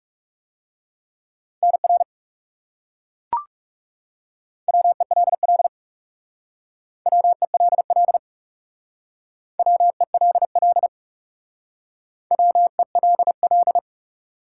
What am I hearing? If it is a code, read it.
Morse code, heard as a keyed tone, about 40 wpm, 700 Hz. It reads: NR WELL WELL WELL WELL